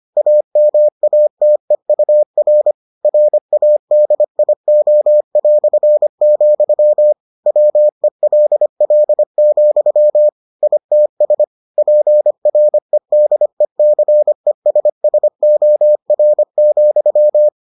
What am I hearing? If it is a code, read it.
Morse code, heard as a keyed tone, about 25 wpm, 600 Hz. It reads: AMATEUR RADIO", WELL, ITS PREDECESSOR,